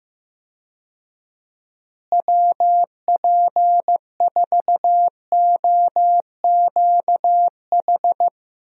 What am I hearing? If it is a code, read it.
Morse code, heard as a keyed tone, about 15 wpm, 700 Hz. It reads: WP4OQH
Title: USA Call Signs - 6 Characters - Repeated 2x - 15wpm